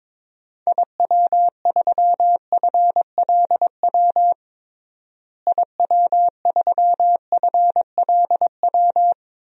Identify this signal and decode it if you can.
Morse code, heard as a keyed tone, about 22 wpm, 700 Hz. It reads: IW3FLW IW3FLW
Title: DXCC-100 Call Signs - 6 Characters - Repeated 3x - 22wpm